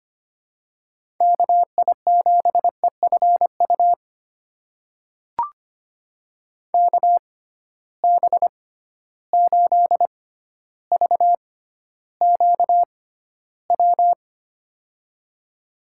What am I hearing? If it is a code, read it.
Morse code, heard as a keyed tone, about 25 wpm, spaced as for 13 wpm, 700 Hz. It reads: KI7EFU KB8VQW
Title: USA Call Signs - 6 Characters - 6x Character Spacing - 25wpm